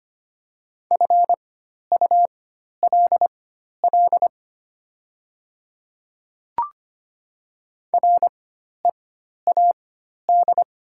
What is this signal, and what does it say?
Morse code, heard as a keyed tone, about 25 wpm, spaced as for 12 wpm, 700 Hz. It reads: FULL READ
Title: 4x Character Spacing - Top 500 Words - 25wpm